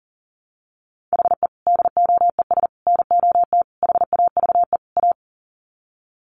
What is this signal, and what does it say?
Morse code, heard as a keyed tone, about 40 wpm, 700 Hz. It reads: HE DOES NOT HAVE A
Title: Sets of 5 Words Encore 40wpm